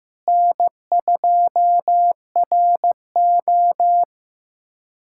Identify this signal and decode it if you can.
Morse code, heard as a keyed tone, about 15 wpm, 700 Hz. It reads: N2RO
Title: USA Callsign DE Callsign - 4 Characters - 15wpm